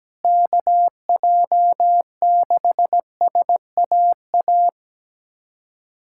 Morse code, heard as a keyed tone, about 17 wpm, 700 Hz. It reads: KJ6SAA
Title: USA Callsign DE Callsign 17wpm